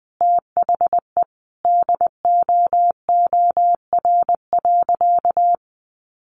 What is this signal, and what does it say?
Morse code, heard as a keyed tone, about 20 wpm, 700 Hz. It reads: THE DOOR.
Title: Sentences from Top 300 Words 20wpm